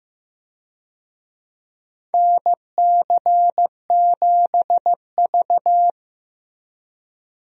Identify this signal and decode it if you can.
Morse code, heard as a keyed tone, about 16 wpm, 700 Hz. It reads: NC7V
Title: USA Call Signs - 4 Characters - Repeated 2x - 15wpm